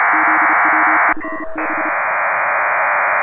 Morse code, heard as a keyed tone, about 32 wpm, 310 Hz, with noise. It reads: ZW5B
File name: Pile-up with PACTOR3